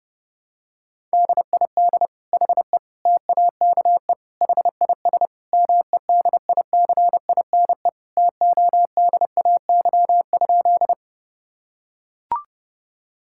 Morse code, heard as a keyed tone, about 30 wpm, 700 Hz. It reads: DID HE TAKE HIS MEDICINE TODAY?
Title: Sentences from Top 1000 Words - Review Part 6 - 30wpm